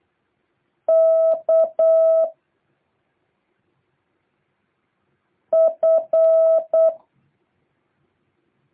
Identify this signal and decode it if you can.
Morse code, heard as a keyed tone, about 8 wpm, 650 Hz. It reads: K F
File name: jh8cbh / 2023cw